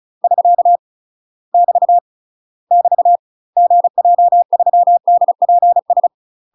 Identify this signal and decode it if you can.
Morse code, heard as a keyed tone, about 34 wpm, 700 Hz. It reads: <SK> <BT> <BT> GJ3DPS
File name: UK MASTER QSOs TRAINING / 35WPM / UK MASTER QSOs 0019